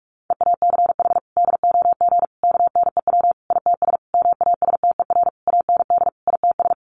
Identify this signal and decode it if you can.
Morse code, heard as a keyed tone, about 44 wpm, 700 Hz. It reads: EACH DOG KNEW ITS MASTER AND ITS
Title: Sentences from Top 1300 Words 45wpm